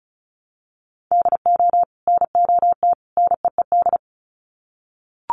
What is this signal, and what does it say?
Morse code, heard as a keyed tone, about 34 wpm, 700 Hz. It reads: DO NOT NEED
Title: Sets of 3 Words 35wpm